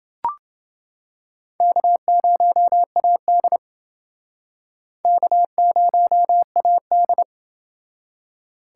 Morse code, heard as a keyed tone, about 29 wpm, 700 Hz. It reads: K0AD K0AD